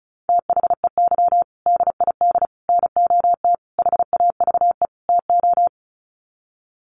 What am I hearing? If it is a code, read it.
Morse code, heard as a keyed tone, about 34 wpm, 700 Hz. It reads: THEY DID NOT HAVE TO
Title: Sets of 5 Words Encore2 35wpm